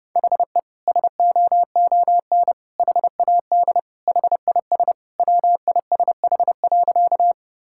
Morse code, heard as a keyed tone, about 30 wpm, 700 Hz. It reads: HE SOON HAD HIS WISH.
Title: Sentences from Top 1600 Words 30wpm